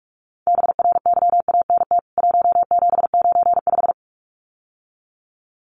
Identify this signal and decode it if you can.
Morse code, heard as a keyed tone, about 44 wpm, 700 Hz. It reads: BRYANT 1795
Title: CWT Exchange 45wpm